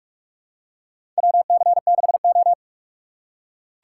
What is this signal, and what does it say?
Morse code, heard as a keyed tone, about 44 wpm, 700 Hz. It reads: WC6O
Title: USA Call Signs - 4 Characters - 2x Character Spacing - 45wpm